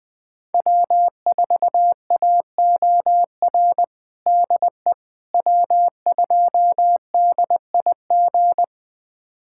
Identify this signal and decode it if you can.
Morse code, heard as a keyed tone, about 20 wpm, 700 Hz. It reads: W4AOR DE W2DIG
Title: USA Callsign DE Callsign - 5 Characters - 20wpm